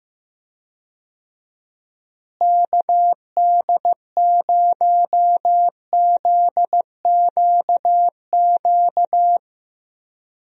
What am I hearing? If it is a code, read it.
Morse code, heard as a keyed tone, about 15 wpm, 700 Hz. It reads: KD0ZQQ
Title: USA Call Signs - 6 Characters - Repeated 2x - 15wpm